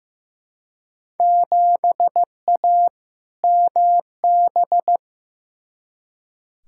Morse code, heard as a keyed tone, about 15 wpm, 700 Hz. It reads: 7A MB
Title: Field Day Exchanges Encore - 15wpm